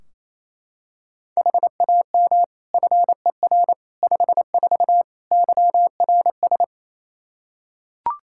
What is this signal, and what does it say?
Morse code, heard as a keyed tone, about 28 wpm, 700 Hz. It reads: HAM FER 54 YRS
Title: Practice QSO - Exchange 3 - More Info - 28wpm